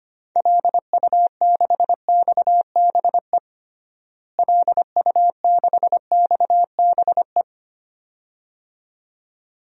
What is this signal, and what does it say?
Morse code, heard as a keyed tone, about 25 wpm, 700 Hz. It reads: LU6XBE LU6XBE
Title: DXCC-100 Call Signs - 6 Characters - Repeated 2x - 25wpm